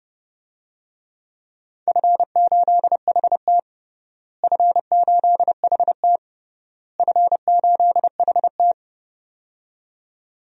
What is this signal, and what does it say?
Morse code, heard as a keyed tone, about 32 wpm, 700 Hz. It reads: F8HT F8HT F8HT
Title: DXCC-100 Call Signs - 4 Characters - Repeated 3x - 30wpm